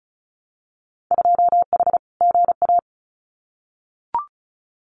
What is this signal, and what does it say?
Morse code, heard as a keyed tone, about 36 wpm, 700 Hz. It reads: 2H GA
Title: Winter Field Day Exchanges Encore 35wpm